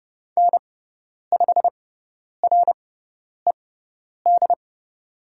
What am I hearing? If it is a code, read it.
Morse code, heard as a keyed tone, about 30 wpm, spaced as for 10 wpm, 700 Hz. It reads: N5RED
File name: Lesson 17g - Letter u - Call Signs 30wpm